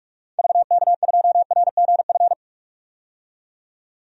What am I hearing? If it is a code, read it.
Morse code, heard as a keyed tone, about 46 wpm, 700 Hz. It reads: VK2RGF
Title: DXCC-100 Call Signs - 6 Characters - 45wpm